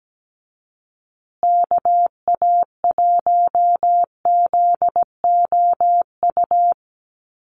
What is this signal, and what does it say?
Morse code, heard as a keyed tone, about 17 wpm, 700 Hz. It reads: KA1ZOU
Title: USA Call Signs Repeated 2x 17wpm